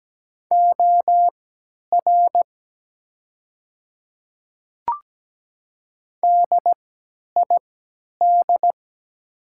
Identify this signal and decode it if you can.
Morse code, heard as a keyed tone, about 17 wpm, spaced as for 10 wpm, 700 Hz. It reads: OR DID
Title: 3x Character Spacing - Top 100 Words - 17wpm